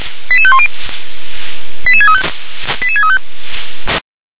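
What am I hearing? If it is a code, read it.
Five-tone selective call (ZVEI), heard as quick runs of five tones, tones of about 70 ms. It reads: 88510 88635 88635
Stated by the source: ZVEI five-tone call 88511 received by SDR.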